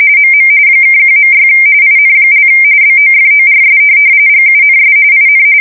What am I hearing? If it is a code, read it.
RTTY, about 45.45 baud. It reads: MM (1.6 IN) BOFORS GUNS IN
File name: fsk170 / USS-Massachusetts